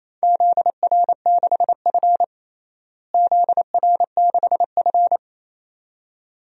Morse code, heard as a keyed tone, about 27 wpm, 700 Hz. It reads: ZR6F ZR6F
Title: DXCC-100 Call Signs - 4 Characters - Repeated 2x - 28wpm